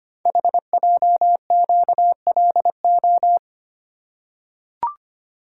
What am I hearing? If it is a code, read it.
Morse code, heard as a keyed tone, about 25 wpm, 700 Hz. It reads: HJQLO
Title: Code Groups - 5 Letters - 25wpm